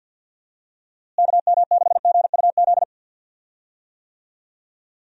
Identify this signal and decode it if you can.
Morse code, heard as a keyed tone, about 51 wpm, 700 Hz. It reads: KM6GUZ
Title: USA Call Signs - 6 Characters - Repeated 2x - 50wpm